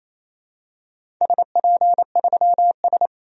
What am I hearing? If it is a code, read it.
Morse code, heard as a keyed tone, about 28 wpm, 700 Hz. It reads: SP3S
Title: DXCC-100 Call Signs - 4 Characters - 28wpm